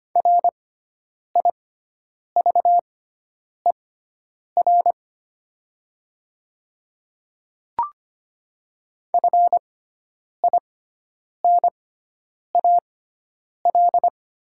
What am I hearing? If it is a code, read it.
Morse code, heard as a keyed tone, about 25 wpm, spaced as for 9 wpm, 700 Hz. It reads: RIVER FINAL
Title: 6x Character Spacing - Top 500 Words - 25wpm